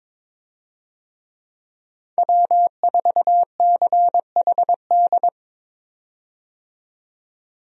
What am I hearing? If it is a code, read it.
Morse code, heard as a keyed tone, about 21 wpm, 700 Hz. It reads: W4CHD
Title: USA Call Signs - 5 Characters - 22wpm